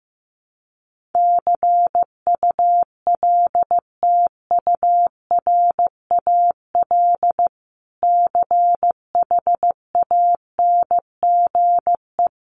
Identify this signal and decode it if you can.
Morse code, heard as a keyed tone, about 15 wpm, 700 Hz. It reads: CULTURAL CHANGE
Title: Sentences from Top 1500 Words 15wpm